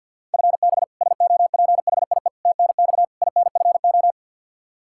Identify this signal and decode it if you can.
Morse code, heard as a keyed tone, about 45 wpm, 700 Hz. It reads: FB SOPHIE TNX INFO
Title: K1USN SST QSO Practice 50wpm